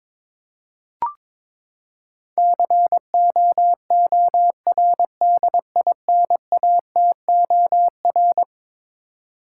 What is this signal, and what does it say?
Morse code, heard as a keyed tone, about 22 wpm, 700 Hz. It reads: COORDINATOR
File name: practice / 22wpm / Eleven Letter Words - 22wpm